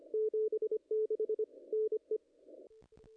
Morse code, heard as a keyed tone, about 24 wpm, 430 Hz. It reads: 76 NE
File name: contests / ss-cw-k5tr-op-n5rz -2300